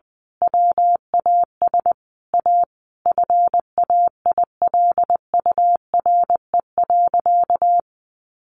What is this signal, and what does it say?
Morse code, heard as a keyed tone, about 20 wpm, 700 Hz. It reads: WAS A FAILURE.
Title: Sentences from Top 900 Words 20wpm